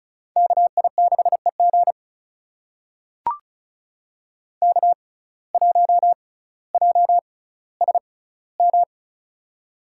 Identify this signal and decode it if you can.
Morse code, heard as a keyed tone, about 36 wpm, spaced as for 18 wpm, 700 Hz. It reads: KI6EG K1JSM
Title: USA Call Signs - 5 Characters - 6x Character Spacing - 35wpm